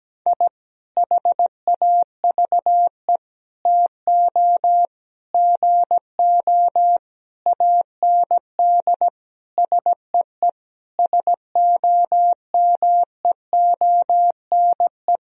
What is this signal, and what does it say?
Morse code, heard as a keyed tone, about 17 wpm, 700 Hz. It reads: I HAVE TO GO AND SEE SOMEONE
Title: Sentences from Top 1000 Words - Review Part 4 - 17wpm